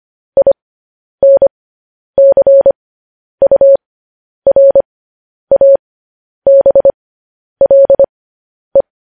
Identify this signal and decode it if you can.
Morse code, heard as a keyed tone, about 25 wpm, spaced as for 10 wpm, 550 Hz. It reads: INCURABLE
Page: news headlines in morse code at 10 wpm